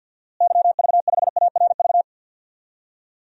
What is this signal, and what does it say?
Morse code, heard as a keyed tone, about 51 wpm, 700 Hz. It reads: YV5ARV
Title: DXCC-100 Call Signs - 6 Characters - Repeated 3x - 50wpm